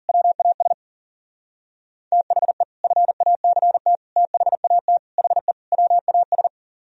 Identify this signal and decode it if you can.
Morse code, heard as a keyed tone, about 52 wpm, 700 Hz. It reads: WAS THE FACT THAT HE WAS